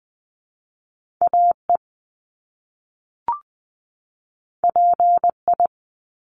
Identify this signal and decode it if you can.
Morse code, heard as a keyed tone, about 20 wpm, 700 Hz. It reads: AE PI
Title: Top Two Letter Combinations in Words 20wpm